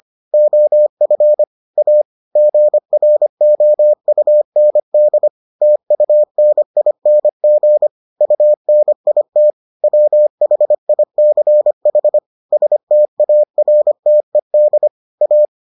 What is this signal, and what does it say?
Morse code, heard as a keyed tone, about 25 wpm, 600 Hz. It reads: OF A GROUND TUNING UNIT WHICH STARTED A